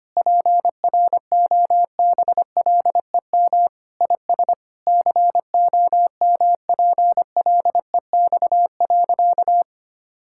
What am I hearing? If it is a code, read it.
Morse code, heard as a keyed tone, about 25 wpm, 700 Hz. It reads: PROBLEM IS COMPLEX.
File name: practice / 25wpm / Sentences with 4-Word Phrases - 25wpm